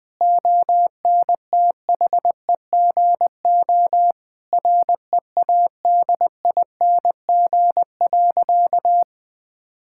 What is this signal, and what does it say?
Morse code, heard as a keyed tone, about 20 wpm, 700 Hz. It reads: ONTHEGO READING.
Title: Difficult Sentences with Hints - 20wpm